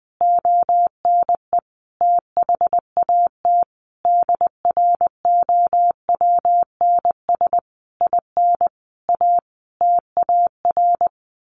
Morse code, heard as a keyed tone, about 20 wpm, 700 Hz. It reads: ONE THAT DROWNS IN A TAR
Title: A Sound Of Thunder 20wpm